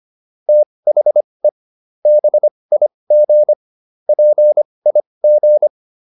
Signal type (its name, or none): Morse code